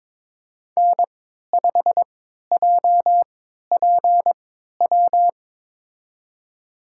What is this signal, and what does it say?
Morse code, heard as a keyed tone, about 22 wpm, spaced as for 13 wpm, 700 Hz. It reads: N5JPW
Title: USA Call Signs - 5 Characters - 3x Character Spacing - 22wpm